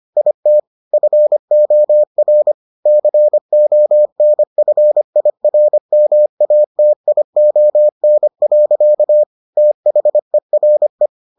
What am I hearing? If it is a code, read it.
Morse code, heard as a keyed tone, about 25 wpm, 600 Hz. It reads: IT FOR CONFIRMATION. THERE